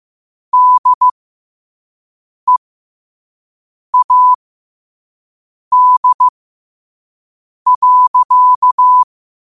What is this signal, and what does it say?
Morse code, heard as a keyed tone, about 15 wpm, spaced as for 5 wpm, 1000 Hz. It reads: DEAD.
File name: qotdfiles / 5wpm